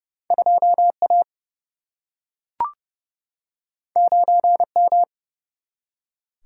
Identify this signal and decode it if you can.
Morse code, heard as a keyed tone, about 31 wpm, 700 Hz. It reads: 2A 9M
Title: Single-Letter Single-Number 30wpm